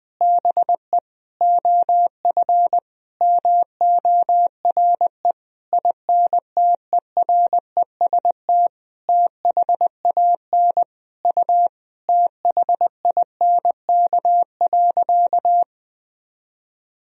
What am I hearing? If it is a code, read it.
Morse code, heard as a keyed tone, about 20 wpm, 700 Hz. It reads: BE OF MORE INTEREST THAN U THINK.
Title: Sentences from Top 400 Words and Abbreviations 20wpm